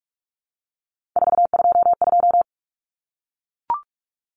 Morse code, heard as a keyed tone, about 44 wpm, 700 Hz. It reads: 422
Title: Numbers - 3-Digit 45wpm